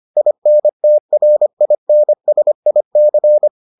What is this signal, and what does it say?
Morse code, heard as a keyed tone, about 25 wpm, 600 Hz. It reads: INTRINSIC